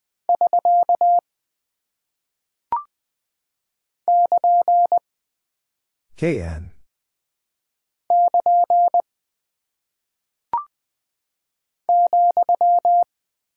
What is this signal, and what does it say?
Morse code, heard as a keyed tone, about 20 wpm, 700 Hz. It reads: <SK> <KN> <KN> ,
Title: Single Punctuation-Prosign - 20wpm